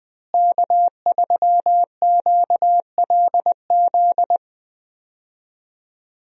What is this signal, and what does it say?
Morse code, heard as a keyed tone, about 20 wpm, 700 Hz. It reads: K3QLZ
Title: USA Call Signs - 5 Characters - Repeated 2x - 20wpm